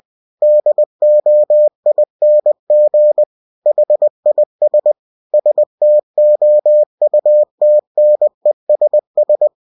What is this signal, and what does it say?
Morse code, heard as a keyed tone, about 20 wpm, 600 Hz. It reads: DOING HIS STOUTNESS